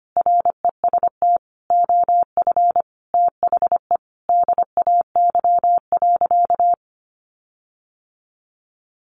Morse code, heard as a keyed tone, about 25 wpm, 700 Hz. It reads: REST OF THE DAY.
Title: Sentences from Top 1400 Words 25wpm